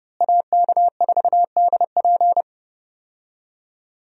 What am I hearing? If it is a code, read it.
Morse code, heard as a keyed tone, about 30 wpm, 700 Hz. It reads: AK4DP
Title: USA Call Signs - 5 Characters - Repeated 3x - 30wpm